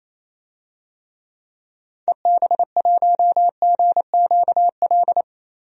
Morse code, heard as a keyed tone, about 28 wpm, 700 Hz. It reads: EB1GQL
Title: DXCC-100 Call Signs - 6 Characters - 28wpm